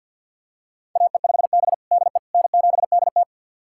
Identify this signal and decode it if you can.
Morse code, heard as a keyed tone, about 51 wpm, 700 Hz. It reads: AE5B DE N7DT
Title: USA Callsign DE Callsign - 4 Characters - 50wpm